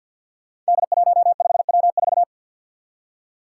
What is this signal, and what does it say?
Morse code, heard as a keyed tone, about 53 wpm, 700 Hz. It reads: D15W4
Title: Five Challenging Letter-Number Combinations 50wpm